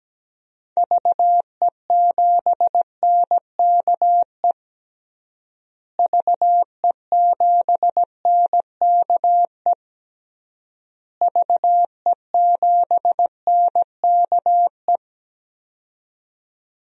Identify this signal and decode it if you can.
Morse code, heard as a keyed tone, about 17 wpm, 700 Hz. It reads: VE7NKE VE7NKE VE7NKE
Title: DXCC-100 Call Signs - 6 Characters - Repeated 3x - 17wpm